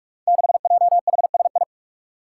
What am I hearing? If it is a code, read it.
Morse code, heard as a keyed tone, about 44 wpm, 700 Hz. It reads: 6JHSI
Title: Five Challenging Letter-Number Combinations 45wpm